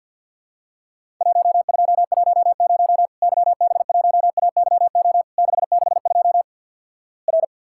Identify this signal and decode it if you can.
Morse code, heard as a keyed tone, about 50 wpm, 700 Hz. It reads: 1210 YB1AYO 662 R
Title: Worked All Europe DX Contest - QTC Exchange - 50wpm